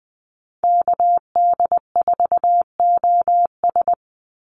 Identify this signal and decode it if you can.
Morse code, heard as a keyed tone, about 20 wpm, 700 Hz. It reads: KD4OS
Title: Call Signs - 20wpm